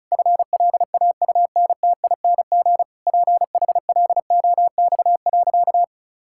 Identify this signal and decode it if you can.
Morse code, heard as a keyed tone, about 36 wpm, 700 Hz. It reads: FLAUNTING PHLOX.
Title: Pangram Challenge v2 - 35wpm